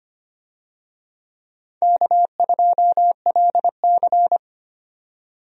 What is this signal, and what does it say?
Morse code, heard as a keyed tone, about 25 wpm, 700 Hz. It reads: K2LC
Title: USA Call Signs - 4 Characters - 25wpm